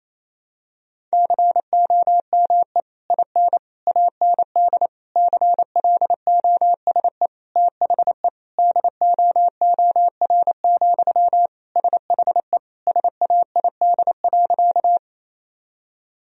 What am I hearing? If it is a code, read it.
Morse code, heard as a keyed tone, about 28 wpm, 700 Hz. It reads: COME IN AND CLOSE THE DOOR, SHE SAID.
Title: Sentences from Top 1000 Words - Review Part 1 - 28wpm